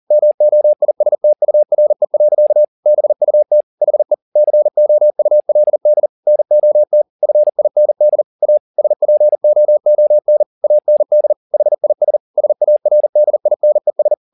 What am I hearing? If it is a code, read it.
Morse code, heard as a keyed tone, about 40 wpm, 600 Hz. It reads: MOISTURE. BUT HE COULD NOT FIND A SPOON AND HIS SARDINES
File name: the old man and the sea 2 40wpm